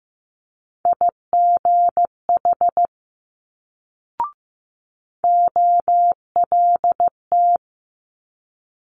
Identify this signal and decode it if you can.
Morse code, heard as a keyed tone, about 15 wpm, 700 Hz. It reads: IGH OLT
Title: Multi-letter Phonemes Quick Fire 15wpm